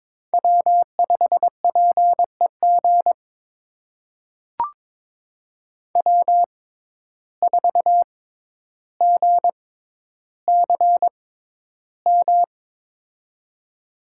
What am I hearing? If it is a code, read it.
Morse code, heard as a keyed tone, about 22 wpm, spaced as for 11 wpm, 700 Hz. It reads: W5PEG W4GCM
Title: USA Call Signs - 5 Characters - 6x Character Spacing - 22wpm